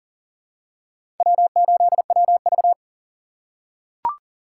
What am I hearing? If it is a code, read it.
Morse code, heard as a keyed tone, about 40 wpm, 700 Hz. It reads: W8WV
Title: USA Call Signs - 4 Characters - 3x Character Spacing - 40wpm